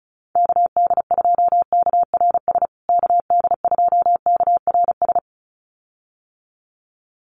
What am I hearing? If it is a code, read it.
Morse code, heard as a keyed tone, about 34 wpm, 700 Hz. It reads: KD2KRS KD2KRS
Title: USA Call Signs Repeated 2x 35wpm